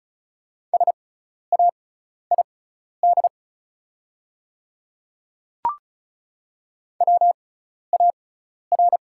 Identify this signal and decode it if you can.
Morse code, heard as a keyed tone, about 36 wpm, spaced as for 12 wpm, 700 Hz. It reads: SAID WAR
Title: 6x Character Spacing - Top 500 Words - 35wpm